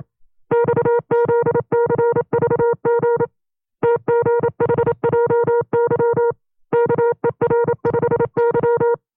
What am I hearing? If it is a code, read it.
Morse code, heard as a keyed tone, about 27 wpm, 470 Hz. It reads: XZCVG TGHJY KER5Y
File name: 25WPM